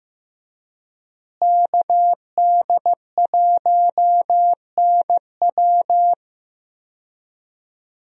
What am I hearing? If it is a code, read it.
Morse code, heard as a keyed tone, about 15 wpm, 700 Hz. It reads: KD1NW
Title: USA Call Signs - 5 Characters - Repeated 3x - 15wpm